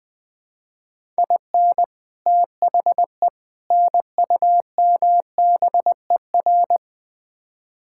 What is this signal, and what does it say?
Morse code, heard as a keyed tone, about 21 wpm, 700 Hz. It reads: IN THE NUMBER
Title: Build A Phrase - 4-Word Phrases - 20wpm